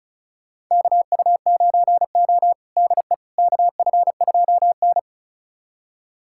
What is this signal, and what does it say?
Morse code, heard as a keyed tone, about 36 wpm, 700 Hz. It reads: KU9O DE KF2N
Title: USA Callsign DE Callsign - 4 Characters - 35wpm